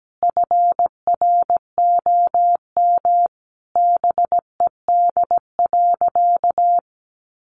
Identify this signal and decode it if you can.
Morse code, heard as a keyed tone, about 17 wpm, 700 Hz. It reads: FROM BED.
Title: Sentences from Top 900 Words 17wpm